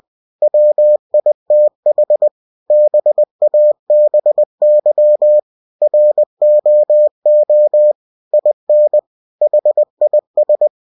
Morse code, heard as a keyed tone, about 20 wpm, 600 Hz. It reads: WITH BABY ROO IN HIS
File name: cw-books / winnie-the-pooh / winnie-20wpm-page-0071